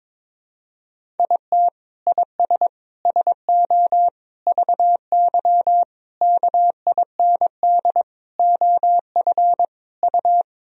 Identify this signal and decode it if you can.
Morse code, heard as a keyed tone, about 22 wpm, 700 Hz. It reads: IT IS SO VY KIND OF U
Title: Sentences from Top 300 Words and Abbreviations 22wpm